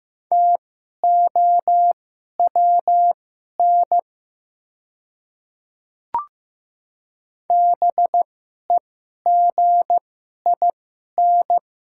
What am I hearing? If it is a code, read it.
Morse code, heard as a keyed tone, about 15 wpm, spaced as for 11 wpm, 690 Hz. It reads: TOWN BEGIN
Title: 2x Character Spacing - Top 500 Words - 15wpm